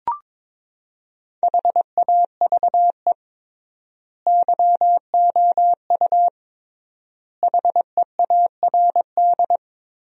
Morse code, heard as a keyed tone, about 22 wpm, 700 Hz. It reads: HAVE YOU HEARD
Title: 3x Word-Spacing - Sentences from Top 400 Words - 22wpm